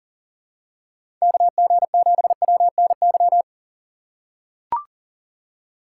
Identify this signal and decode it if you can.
Morse code, heard as a keyed tone, about 40 wpm, 700 Hz. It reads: KG7WNY
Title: USA Call Signs - 6 Characters - 3x Character Spacing - 40wpm